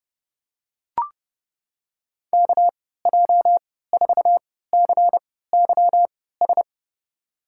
Morse code, heard as a keyed tone, about 32 wpm, spaced as for 17 wpm, 700 Hz. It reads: KJ4CYS
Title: USA Call Signs - 3x Character Spacing - 30wpm